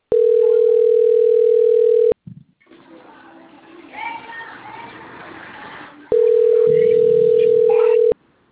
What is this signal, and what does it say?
Ringback tone (North America), 440 + 480 Hz.